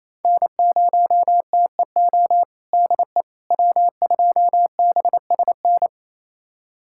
Morse code, heard as a keyed tone, about 28 wpm, 700 Hz. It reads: N0TEO DE W2BSN
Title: USA Callsign DE Callsign - 5 Characters - 28wpm